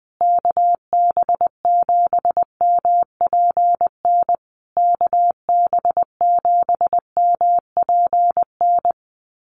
Morse code, heard as a keyed tone, about 20 wpm, 700 Hz. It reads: KB7MPN KB7MPN
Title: USA Call Signs Repeated 3x 20wpm